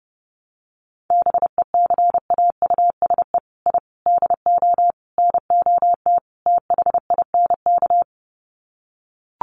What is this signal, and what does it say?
Morse code, heard as a keyed tone, about 30 wpm, 700 Hz. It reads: BECAUSE I DO NOT THINK
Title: Sets of 5 Words Encore 30wpm